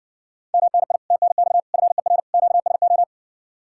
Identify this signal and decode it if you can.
Morse code, heard as a keyed tone, about 66 wpm, 700 Hz. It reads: MNI TNX FER QSO